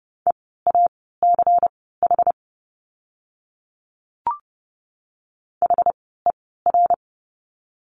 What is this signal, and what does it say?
Morse code, heard as a keyed tone, about 30 wpm, spaced as for 17 wpm, 700 Hz. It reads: EACH HER
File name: course / 30:10wpm / 115 Lesson 19f - Letter w - Words 30wpm